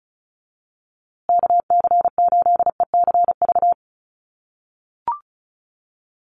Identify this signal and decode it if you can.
Morse code, heard as a keyed tone, about 34 wpm, 700 Hz. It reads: KC8ECV
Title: USA Call Signs Repeated 2x 35wpm